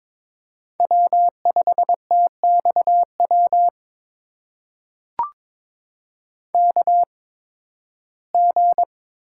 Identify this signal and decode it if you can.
Morse code, heard as a keyed tone, about 22 wpm, spaced as for 14 wpm, 700 Hz. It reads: W5TXW KG
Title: USA Call Signs - 5 Characters - 8x Character Spacing - 22wpm